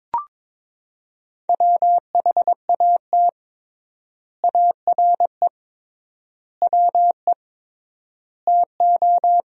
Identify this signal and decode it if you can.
Morse code, heard as a keyed tone, about 22 wpm, 700 Hz. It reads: WHAT ARE WE TO
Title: 3x Word-Spacing - Sentences from Top 100 Words - 22wpm